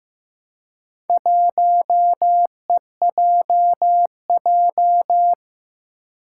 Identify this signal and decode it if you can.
Morse code, heard as a keyed tone, about 15 wpm, 700 Hz. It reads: 1EJJ
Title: Four Challenging Letter-Number Combinations 15wpm